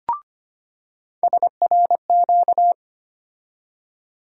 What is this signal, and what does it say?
Morse code, heard as a keyed tone, about 25 wpm, 700 Hz. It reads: SRQ
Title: USA Call Sign Postfixes - 3 Letters 25wpm